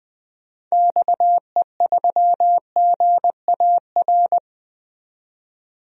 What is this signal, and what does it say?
Morse code, heard as a keyed tone, about 20 wpm, 700 Hz. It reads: XE3GAR
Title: DXCC-100 Call Signs - 6 Characters - 20wpm